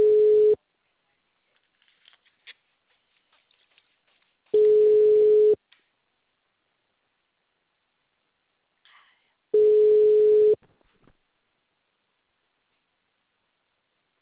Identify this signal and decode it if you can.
Ringback tone (Europe, 425 Hz).